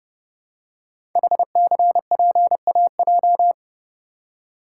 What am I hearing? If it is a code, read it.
Morse code, heard as a keyed tone, about 30 wpm, 690 Hz. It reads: HCPAJ
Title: Code Groups - 5 Letters - 30wpm